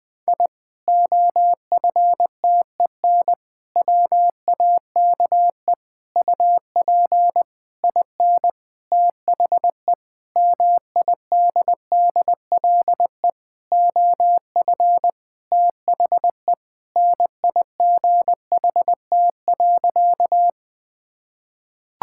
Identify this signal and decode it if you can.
Morse code, heard as a keyed tone, about 20 wpm, 700 Hz. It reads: I OFTEN WAKE UP IN THE MIDDLE OF THE NIGHT.
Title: Sentences with 8-Word Phrases - 20wpm